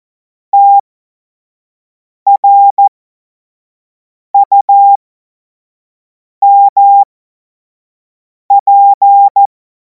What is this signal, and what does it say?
Morse code, heard as a keyed tone, about 14 wpm, spaced as for 5 wpm, 800 Hz. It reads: TRUMP
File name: Morsekurs / Witze / Witz 26. Mai 2026 25 F70